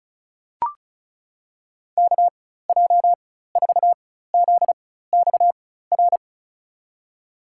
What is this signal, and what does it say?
Morse code, heard as a keyed tone, about 34 wpm, spaced as for 16 wpm, 700 Hz. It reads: KJ4ZXR
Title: USA Call Signs - 6 Characters - 4x Character Spacing - 35wpm